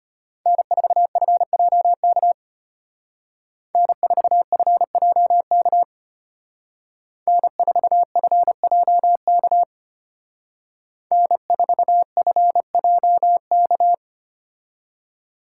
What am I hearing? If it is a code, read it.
Morse code, heard as a keyed tone, about 29 wpm, 700 Hz. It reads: N4FJK N4FJK N4FJK N4FJK
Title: Speed Racing - USA 1x3 Call Signs 25wpm